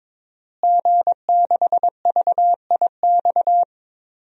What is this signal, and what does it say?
Morse code, heard as a keyed tone, about 22 wpm, 700 Hz. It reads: G6VIX
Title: DXCC-100 Call Signs - 5 Characters - Repeated 2x - 22wpm